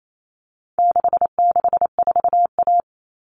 Morse code, heard as a keyed tone, about 28 wpm, 700 Hz. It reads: 664A
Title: All Cut Numbers - 4-Digit 28wpm